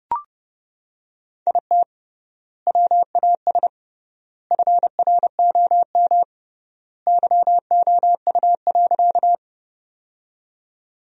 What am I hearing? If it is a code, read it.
Morse code, heard as a keyed tone, about 30 wpm, 700 Hz. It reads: IT WAS FROM YOU.
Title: 3x Word-Spacing - Sentences from Top 100 Words - 30wpm